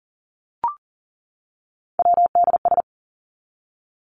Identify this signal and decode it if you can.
Morse code, heard as a keyed tone, about 40 wpm, 700 Hz. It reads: WDS
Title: Abbreviations 40wpm